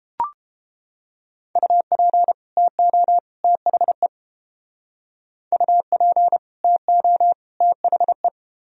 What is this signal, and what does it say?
Morse code, heard as a keyed tone, about 28 wpm, 700 Hz. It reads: UP TO THE UP TO THE